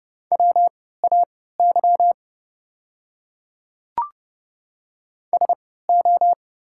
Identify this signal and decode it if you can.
Morse code, heard as a keyed tone, about 32 wpm, spaced as for 17 wpm, 690 Hz. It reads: WAY SO
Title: 3x Character Spacing - Top 200 Words - 30wpm